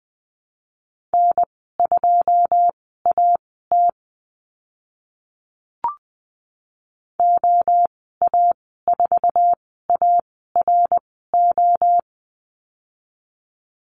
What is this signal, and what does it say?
Morse code, heard as a keyed tone, about 20 wpm, spaced as for 15 wpm, 700 Hz. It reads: N2AT OA4ARO
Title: 082 Lesson 14g - Number 2 - Call Signs 20wpm